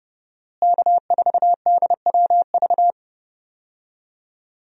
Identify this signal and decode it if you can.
Morse code, heard as a keyed tone, about 30 wpm, 700 Hz. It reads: K4DWV
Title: USA Call Signs - 5 Characters - Repeated 2x - 30wpm